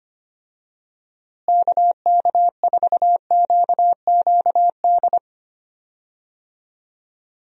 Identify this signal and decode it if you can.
Morse code, heard as a keyed tone, about 25 wpm, 700 Hz. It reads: KK4QQD
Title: USA Call Signs - 6 Characters - Repeated 2x - 25wpm